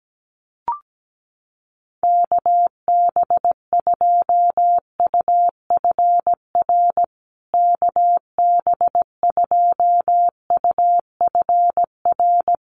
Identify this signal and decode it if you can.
Morse code, heard as a keyed tone, about 17 wpm, 700 Hz. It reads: KB2UFR KB2UFR
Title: USA Call Signs Repeated 3x 17wpm